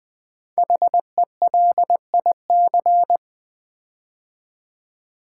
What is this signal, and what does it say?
Morse code, heard as a keyed tone, about 20 wpm, 700 Hz. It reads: HELIC
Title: Build A Word - 5 Letters At A Time - 10-Letter Words - 20wpm